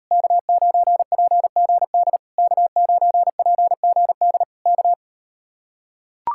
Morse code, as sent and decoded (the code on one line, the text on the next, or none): -.- ----. .--. --. -.. / -.- ----. .--. --. -.. / -.-
K9PGD K9PGD K